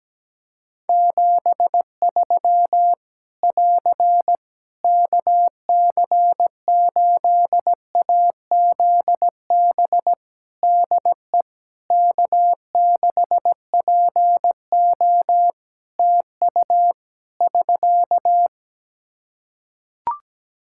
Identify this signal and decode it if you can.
Morse code, heard as a keyed tone, about 17 wpm, 700 Hz. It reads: 73 <AR> KC8AZB DE K6PO TU <SK>
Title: Practice QSO - Exchange 4 - Ending - 17wpm